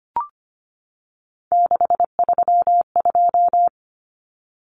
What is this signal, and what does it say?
Morse code, heard as a keyed tone, about 25 wpm, 700 Hz. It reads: 632
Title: Common Cut Numbers - 3-Digit 25wpm